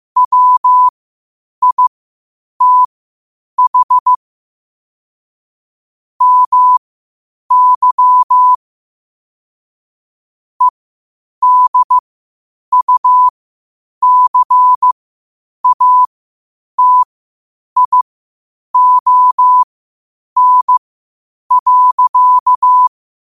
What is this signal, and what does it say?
Morse code, heard as a keyed tone, about 15 wpm, spaced as for 8 wpm, 1000 Hz. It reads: WITH MY EDUCATION.
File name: qotdfiles / 8wpm